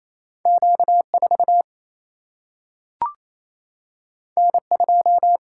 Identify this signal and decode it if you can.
Morse code, heard as a keyed tone, about 28 wpm, 700 Hz. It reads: Q4 N2
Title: Single-Letter Single-Number 28wpm